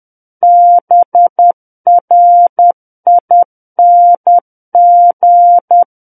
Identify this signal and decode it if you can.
Morse code, heard as a keyed tone, about 10 wpm, 700 Hz. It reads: BRING
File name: Book One / 01-Chapter 1 - 10 WPM